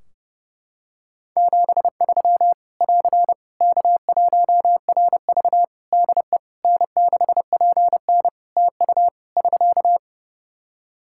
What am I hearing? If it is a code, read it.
Morse code, heard as a keyed tone, about 30 wpm, 700 Hz. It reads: 73 <AR> K1RV DE N6PN TU <SK>
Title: Practice QSO - Exchange 4 - Ending - 30wpm